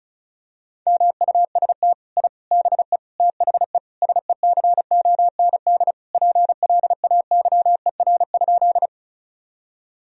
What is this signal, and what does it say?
Morse code, heard as a keyed tone, about 34 wpm, 700 Hz. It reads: MUST I BE THE SECOND PLAYER?
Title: Sentences from Top 1000 Words - Review Part 4 - 35wpm